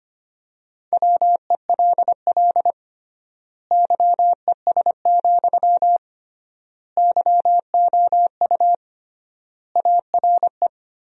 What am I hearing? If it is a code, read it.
Morse code, heard as a keyed tone, about 25 wpm, 700 Hz. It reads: WELL YES, YOU ARE
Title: 3x Word-Spacing - Sentences from Top 500 Words - 25wpm